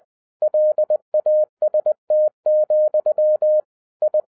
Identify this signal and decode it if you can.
Morse code, heard as a keyed tone, about 20 wpm, 610 Hz. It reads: LAST, I